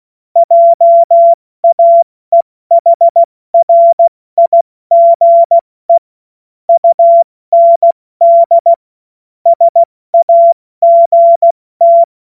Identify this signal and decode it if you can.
Morse code, heard as a keyed tone, about 16 wpm, 680 Hz. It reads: JAEHRIGE UND SAGT
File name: Witz-08-Juli-2025-70-F80